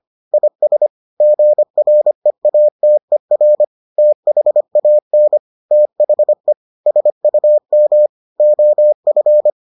Morse code, heard as a keyed tone, about 25 wpm, 600 Hz. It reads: IS GREATER THAN THE SUM OF